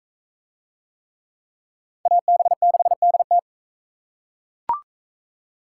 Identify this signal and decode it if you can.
Morse code, heard as a keyed tone, about 41 wpm, 700 Hz. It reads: AB6DT